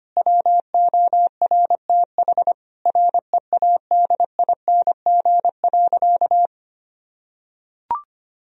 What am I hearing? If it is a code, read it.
Morse code, heard as a keyed tone, about 25 wpm, 700 Hz. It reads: WORTH READING.
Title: Sentences from Top 1000 Words - Review Part 3 - 25wpm